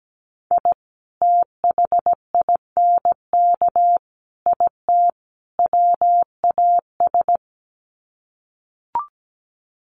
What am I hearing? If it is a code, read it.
Morse code, heard as a keyed tone, about 17 wpm, 700 Hz. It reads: I THINK IT WAS
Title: Sets of 4 Words 17wpm